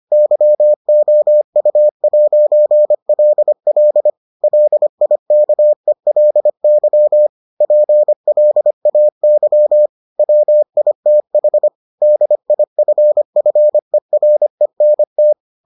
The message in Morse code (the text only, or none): YOU'LL LIKELY PLAY WITH DIFFERENT